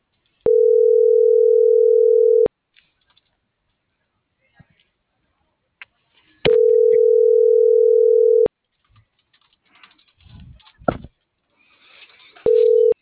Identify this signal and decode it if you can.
Ringback tone (North America), 440 + 480 Hz.